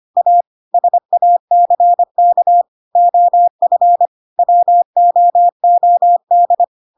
Morse code, heard as a keyed tone, about 25 wpm, 700 Hz. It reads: A SACK OF WOOD
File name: Pinocchio / Pinocchio ch7 25 25 ik1hge